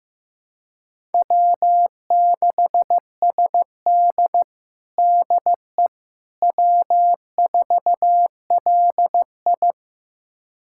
Morse code, heard as a keyed tone, about 15 wpm, 700 Hz. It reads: W6SD DE W4LI
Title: USA Callsign DE Callsign - 4 Characters - 15wpm